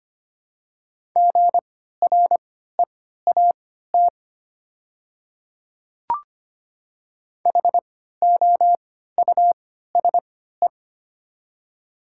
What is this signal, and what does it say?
Morse code, heard as a keyed tone, about 25 wpm, spaced as for 14 wpm, 700 Hz. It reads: GREAT HOUSE
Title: 3x Character Spacing - Top 200 Words - 25wpm